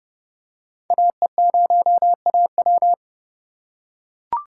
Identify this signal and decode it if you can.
Morse code, heard as a keyed tone, about 32 wpm, 700 Hz. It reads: AE0AW
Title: USA Call Signs - 5 Characters - 4x Character Spacing - 30wpm